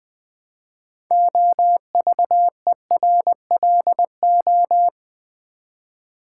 Morse code, heard as a keyed tone, about 21 wpm, 700 Hz. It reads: OVERLO
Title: Build A Word - 2 Letters At A Time - 8-Letter Words - 20wpm